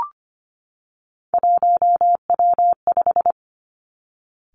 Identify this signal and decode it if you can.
Morse code, heard as a keyed tone, about 25 wpm, 700 Hz. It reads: 1W5
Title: Three Challenging Letter-Number Combinations 25wpm